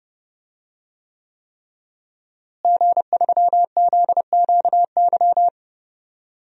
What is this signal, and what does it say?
Morse code, heard as a keyed tone, about 31 wpm, 700 Hz. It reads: G3ZQY
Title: DXCC-100 Call Signs - 5 Characters - Repeated 3x - 30wpm